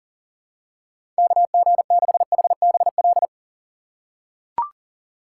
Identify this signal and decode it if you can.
Morse code, heard as a keyed tone, about 37 wpm, 700 Hz. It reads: KG6HBL
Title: USA Call Signs - 6 Characters - 6x Character Spacing - 40wpm